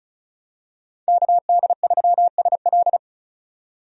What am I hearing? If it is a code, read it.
Morse code, heard as a keyed tone, about 36 wpm, 700 Hz. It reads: KD3SL
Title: USA Call Signs - 5 Characters - 35wpm